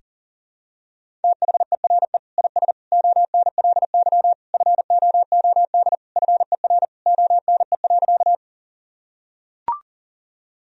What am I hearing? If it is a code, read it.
Morse code, heard as a keyed tone, about 40 wpm, 700 Hz. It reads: THERE IS ONLY FOOD FER ONE.
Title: Sentences from Top 300 Words and Abbreviations 40wpm